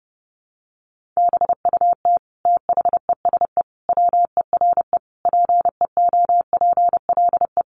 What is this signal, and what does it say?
Morse code, heard as a keyed tone, about 30 wpm, 700 Hz. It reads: BUT THESE WERE PEOPLE
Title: Sentences from Top 1500 Words 30wpm